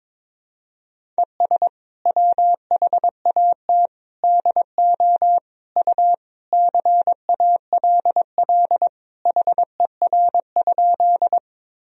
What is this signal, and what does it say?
Morse code, heard as a keyed tone, about 22 wpm, 700 Hz. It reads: ES WHAT DO U CALL HER?
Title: Sentences from Top 100 Words and Abbreviations 22wpm